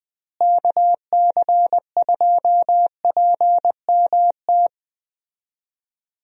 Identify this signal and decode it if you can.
Morse code, heard as a keyed tone, about 20 wpm, 700 Hz. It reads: KC2PMT
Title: USA Call Signs - 6 Characters - Repeated 2x - 20wpm